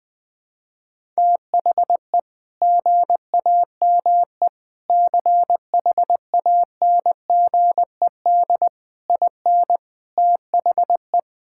Morse code, heard as a keyed tone, about 20 wpm, 700 Hz. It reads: THE GAME CHANGED IN THE
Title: Sentences with 5-Word Phrases - Spoken Beginning - 20wpm